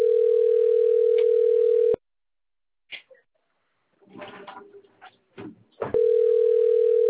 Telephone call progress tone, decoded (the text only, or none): ringback tone